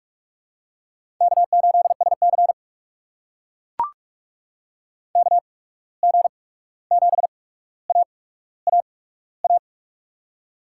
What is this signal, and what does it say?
Morse code, heard as a keyed tone, about 44 wpm, spaced as for 17 wpm, 700 Hz. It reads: K8SC KG7AAA
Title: USA Call Signs - 8x Character Spacing - 45wpm